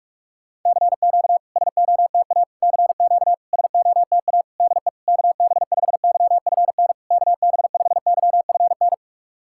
Morse code, heard as a keyed tone, about 46 wpm, 700 Hz. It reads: CQ SOTA CQ SOTA DE KB5YFN KB5YFN
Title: SOTA QSO Practice 45wpm